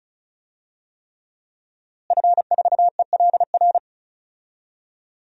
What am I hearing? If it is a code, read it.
Morse code, heard as a keyed tone, about 36 wpm, 700 Hz. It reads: F4ELR
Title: DXCC-100 Call Signs - 5 Characters - Repeated 3x - 35wpm